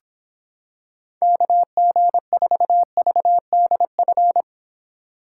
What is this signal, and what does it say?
Morse code, heard as a keyed tone, about 26 wpm, 700 Hz. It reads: KG4VDF